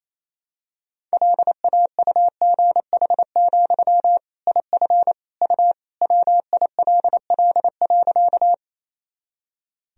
Morse code, heard as a keyed tone, about 28 wpm, 700 Hz. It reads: LAUGH, IF U WILL.
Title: Sentences from Top 500 Words and Abbreviations 28wpm